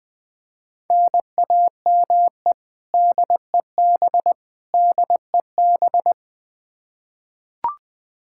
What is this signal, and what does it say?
Morse code, heard as a keyed tone, about 20 wpm, 700 Hz. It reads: NAME DEB DEB
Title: Practice QSO - Exchange 1 - Beginning - 20wpm